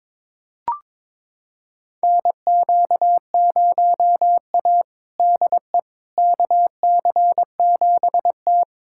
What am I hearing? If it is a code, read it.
Morse code, heard as a keyed tone, about 23 wpm, 700 Hz. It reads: NQ0A DE KC7T
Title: USA Callsign DE Callsign - 4 Characters - 22wpm